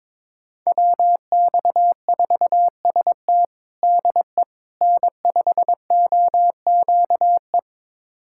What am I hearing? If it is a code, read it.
Morse code, heard as a keyed tone, about 22 wpm, 700 Hz. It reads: WX4ST DE N5OQE
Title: USA Callsign DE Callsign 22wpm